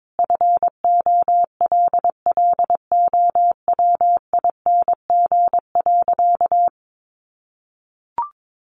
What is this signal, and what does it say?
Morse code, heard as a keyed tone, about 22 wpm, 700 Hz. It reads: FOLLOWING.
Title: Sentences from Top 1300 Words 22wpm